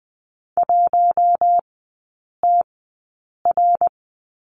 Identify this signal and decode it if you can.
Morse code, heard as a keyed tone, about 20 wpm, spaced as for 8 wpm, 700 Hz. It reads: 1TR
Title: 053 Lesson 10e - Letter r - Three Characters 20wpm